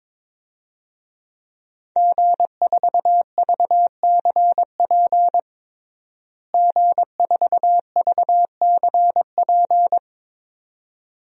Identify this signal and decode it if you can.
Morse code, heard as a keyed tone, about 22 wpm, 700 Hz. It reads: G4VCP G4VCP
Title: DXCC-100 Call Signs - 5 Characters - Repeated 2x - 22wpm